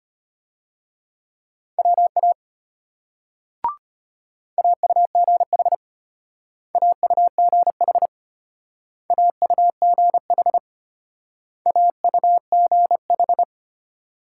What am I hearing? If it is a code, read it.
Morse code, heard as a keyed tone, about 32 wpm, 700 Hz. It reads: WA AUGH AUGH AUGH AUGH